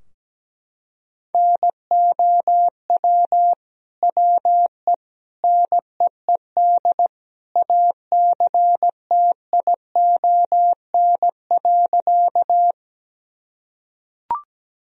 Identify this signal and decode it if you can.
Morse code, heard as a keyed tone, about 17 wpm, 700 Hz. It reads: NOW WE NEED ACTION.
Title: Sentences from Top 1000 Words - Review Part 4 - 17wpm